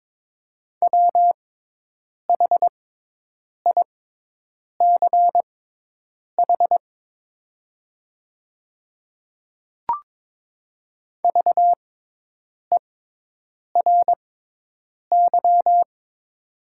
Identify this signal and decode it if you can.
Morse code, heard as a keyed tone, about 22 wpm, spaced as for 8 wpm, 700 Hz. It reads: WHICH VERY
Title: 6x Character Spacing - Top 200 Words - 22wpm